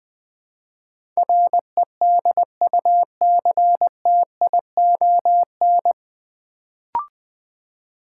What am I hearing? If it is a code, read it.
Morse code, heard as a keyed tone, about 20 wpm, 700 Hz. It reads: REDUCTION
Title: Build A Word - 3 Letters At A Time - 9-Letter Words - 20wpm